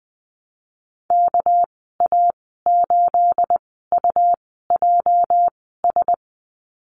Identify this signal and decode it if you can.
Morse code, heard as a keyed tone, about 20 wpm, spaced as for 14 wpm, 700 Hz. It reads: KA8UJS
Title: 213 Lesson 34g - Number 8 - Call Signs 20wpm